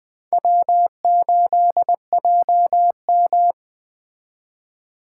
Morse code, heard as a keyed tone, about 20 wpm, 700 Hz. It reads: W8JM
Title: Speed Racing - USA 1x2 Call Signs 17wpm